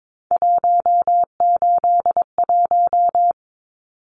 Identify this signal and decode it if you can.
Morse code, heard as a keyed tone, about 22 wpm, 700 Hz. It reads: 181
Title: Numbers - 3-Digit 22wpm